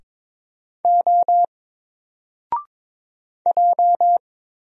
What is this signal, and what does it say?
Morse code, heard as a keyed tone, about 22 wpm, 700 Hz. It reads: O J
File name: Single-Letter-Number-22wpm